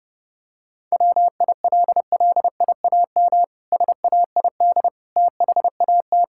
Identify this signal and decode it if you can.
Morse code, heard as a keyed tone, about 28 wpm, 700 Hz. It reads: WILLIAM SAID THAT
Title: Pangram Challenge v2 - 30wpm